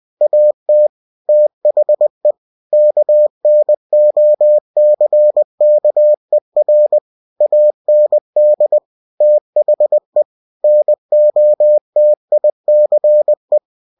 Morse code, heard as a keyed tone, about 20 wpm, 600 Hz. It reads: AT THE KNOCKER AND THE NOTICE
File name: cw-books / winnie-the-pooh / winnie-20wpm-page-0031